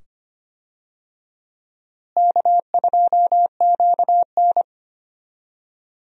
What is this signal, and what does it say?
Morse code, heard as a keyed tone, about 25 wpm, 700 Hz. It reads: K2QN
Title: USA Call Signs - 4 Characters - Repeated 2x - 25wpm